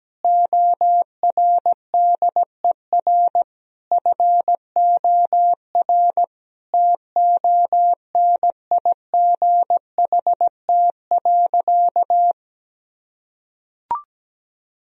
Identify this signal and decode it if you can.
Morse code, heard as a keyed tone, about 17 wpm, 700 Hz. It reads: ORDER FOR TONIGHT.
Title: Sentences from Top 1000 Words - Review Part 3 - 17wpm